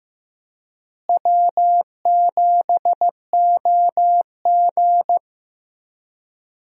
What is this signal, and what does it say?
Morse code, heard as a keyed tone, about 15 wpm, 700 Hz. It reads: W7OG
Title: USA Call Signs - 4 Characters - Repeated 3x - 15wpm